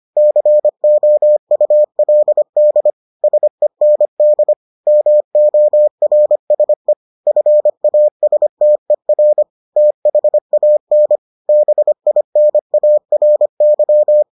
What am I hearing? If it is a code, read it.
Morse code, heard as a keyed tone, about 25 wpm, 600 Hz. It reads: COULD SEND MORSE FASTER THAN BINARY